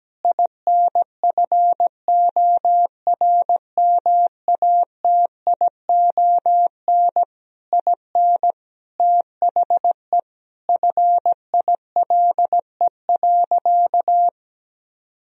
Morse code, heard as a keyed tone, about 17 wpm, 700 Hz. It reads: INFORMATION IN THE FILE.
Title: Sentences from Top 1000 Words - Review Part 4 - 17wpm